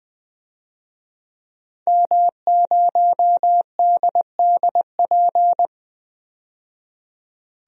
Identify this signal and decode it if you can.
Morse code, heard as a keyed tone, about 20 wpm, 700 Hz. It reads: M0DDP
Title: DXCC-100 Call Signs - 5 Characters - 20wpm